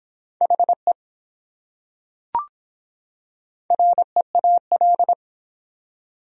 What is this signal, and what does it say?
Morse code, heard as a keyed tone, about 24 wpm, 700 Hz. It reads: HE REAL